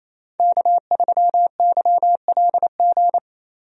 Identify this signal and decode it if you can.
Morse code, heard as a keyed tone, about 28 wpm, 700 Hz. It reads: K3YLG
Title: USA Call Signs - 5 Characters - Repeated 2x - 28wpm